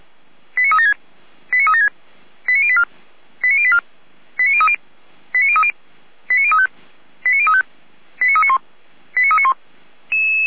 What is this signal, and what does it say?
Five-tone selective call (ZVEI), heard as quick runs of five tones, tones of about 70 ms. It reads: 89376 89376 89084 89084 89030 89030 89035 89035 89391 89391